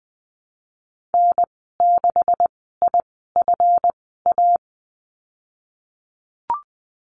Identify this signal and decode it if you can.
Morse code, heard as a keyed tone, about 19 wpm, spaced as for 14 wpm, 700 Hz. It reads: N6IFA
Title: 161 Lesson 26g - Letter g - Call Signs 20wpm